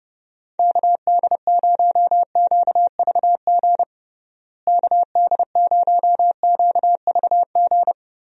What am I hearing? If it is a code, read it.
Morse code, heard as a keyed tone, about 32 wpm, 700 Hz. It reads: KD0QVG KD0QVG
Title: USA Call Signs - 6 Characters - Repeated 3x - 30wpm